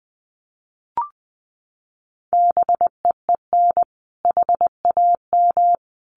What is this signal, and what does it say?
Morse code, heard as a keyed tone, about 19 wpm, 700 Hz. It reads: BEEN HAM
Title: Common QSO Phrases Encore 20wpm